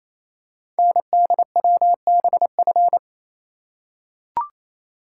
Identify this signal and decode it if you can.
Morse code, heard as a keyed tone, about 29 wpm, 700 Hz. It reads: NDWBF
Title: Code Groups - 5 Letters - 28wpm